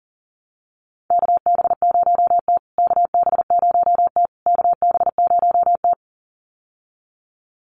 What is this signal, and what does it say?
Morse code, heard as a keyed tone, about 40 wpm, 700 Hz. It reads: KB0T KB0T KB0T
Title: USA Call Signs Repeated 3x 40wpm